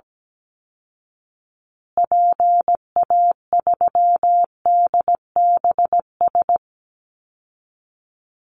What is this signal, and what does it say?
Morse code, heard as a keyed tone, about 17 wpm, 700 Hz. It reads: PA3DBS
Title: Call Signs Repeated 3x - 17wpm